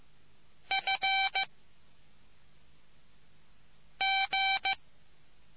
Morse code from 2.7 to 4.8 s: G